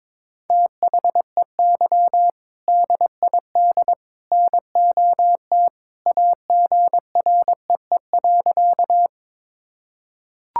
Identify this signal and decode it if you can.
Morse code, heard as a keyed tone, about 22 wpm, 700 Hz. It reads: THEY DID NOT AGREE.
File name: practice / 22wpm / Sentences with 3-Word Phrases - 22wpm